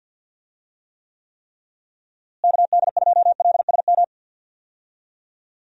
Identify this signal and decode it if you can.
Morse code, heard as a keyed tone, about 47 wpm, 700 Hz. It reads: KD2LSM
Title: USA Call Signs - 6 Characters - Repeated 3x - 50wpm